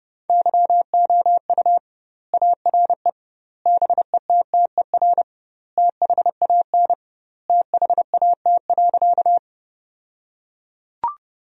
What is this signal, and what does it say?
Morse code, heard as a keyed tone, about 32 wpm, 700 Hz. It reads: YOU ARE BETTER THAN THAT.
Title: 2x Word-Spacing - Sentences from Top 400 Words - 30wpm